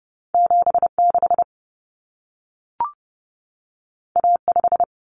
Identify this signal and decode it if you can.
Morse code, heard as a keyed tone, about 30 wpm, 700 Hz. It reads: 76 A5
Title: Common Cut Numbers - 2-Digit 30wpm